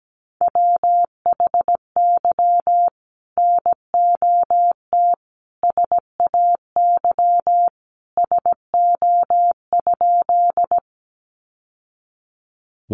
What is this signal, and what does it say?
Morse code, heard as a keyed tone, about 17 wpm, 700 Hz. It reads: WHY NOT SAY SO?
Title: Sentences from Top 600 Words 17wpm